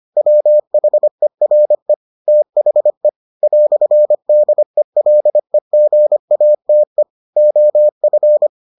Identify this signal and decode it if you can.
Morse code, heard as a keyed tone, about 25 wpm, 600 Hz. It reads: WHERE THE "DELEGATE OF